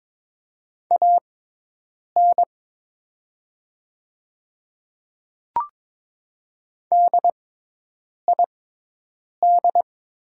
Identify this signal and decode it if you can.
Morse code, heard as a keyed tone, about 22 wpm, spaced as for 8 wpm, 700 Hz. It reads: AN DID
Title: 6x Character Spacing - Top 500 Words - 22wpm